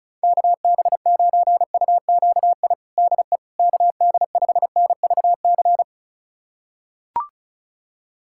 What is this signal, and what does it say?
Morse code, heard as a keyed tone, about 36 wpm, 700 Hz. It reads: KB9UQI DE KD5NVC
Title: USA Callsign DE Callsign - 6 Characters - 35wpm